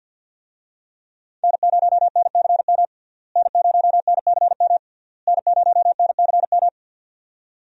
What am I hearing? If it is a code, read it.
Morse code, heard as a keyed tone, about 53 wpm, 700 Hz. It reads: N0NCM N0NCM N0NCM
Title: USA Call Signs - 5 Characters - Repeated 3x - 50wpm